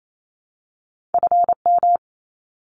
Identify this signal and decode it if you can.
Morse code, heard as a keyed tone, about 28 wpm, 700 Hz. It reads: FM
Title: Abbreviations 28wpm